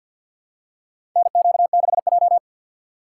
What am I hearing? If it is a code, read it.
Morse code, heard as a keyed tone, about 48 wpm, 700 Hz. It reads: NQ6J
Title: USA Call Signs - 4 Characters - 3x Character Spacing - 50wpm